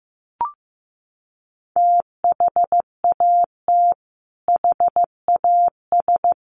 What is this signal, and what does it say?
Morse code, heard as a keyed tone, about 15 wpm, 700 Hz. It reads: THAT HAS
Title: Sentences from Top 700 Words 15wpm